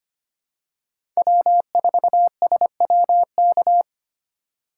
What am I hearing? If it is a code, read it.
Morse code, heard as a keyed tone, about 25 wpm, 700 Hz. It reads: W4SWK
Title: USA Call Signs - 2x Character Spacing - 25wpm